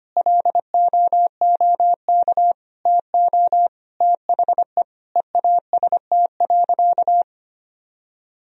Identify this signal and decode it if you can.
Morse code, heard as a keyed tone, about 25 wpm, 700 Hz. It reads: LOOK TO THE EAST.
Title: Sentences from Top 1000 Words - Review Part 3 - 25wpm